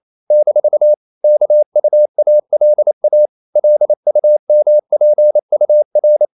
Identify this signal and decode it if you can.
Morse code, heard as a keyed tone, about 28 wpm, 600 Hz. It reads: - KUALA LUMPUR
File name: CW / Practice / NationalCapitals28